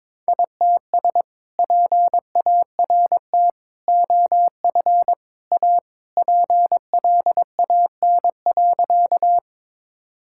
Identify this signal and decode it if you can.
Morse code, heard as a keyed tone, about 21 wpm, 700 Hz. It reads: ITS PART OF A PLAN.
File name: practice / 22wpm / Sentences with 3-Word Phrases - 22wpm